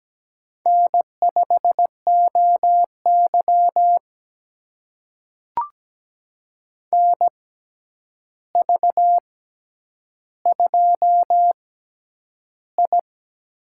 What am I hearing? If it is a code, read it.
Morse code, heard as a keyed tone, about 17 wpm, spaced as for 9 wpm, 700 Hz. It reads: N5OY NV2I
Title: USA Call Signs - 4 Characters - 6x Character Spacing - 17wpm